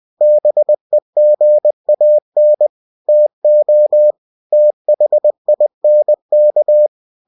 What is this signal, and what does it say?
Morse code, heard as a keyed tone, about 20 wpm, 600 Hz. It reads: BEGAN TO THINK
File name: cw-books / winnie-the-pooh / winnie-20wpm-page-0004